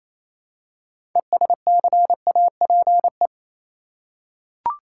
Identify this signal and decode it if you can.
Morse code, heard as a keyed tone, about 28 wpm, 700 Hz. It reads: ESCAPE
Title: Six Letter Words - 28wpm